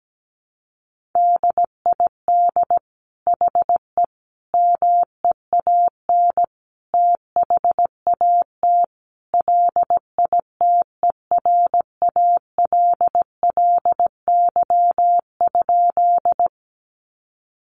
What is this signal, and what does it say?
Morse code, heard as a keyed tone, about 17 wpm, 700 Hz. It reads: DID HE MEAN THAT LITERALLY?
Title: Sentences from Top 1400 Words 17wpm